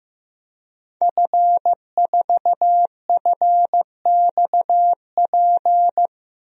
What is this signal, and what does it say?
Morse code, heard as a keyed tone, about 15 wpm, 700 Hz. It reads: F4FXP
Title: DXCC-100 Call Signs - 5 Characters - 15wpm